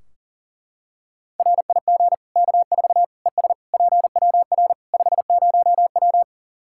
Morse code, heard as a keyed tone, about 38 wpm, 700 Hz. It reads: RIG K4 ES PWR 50W
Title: Practice QSO - Exchange 2 - Info - 40wpm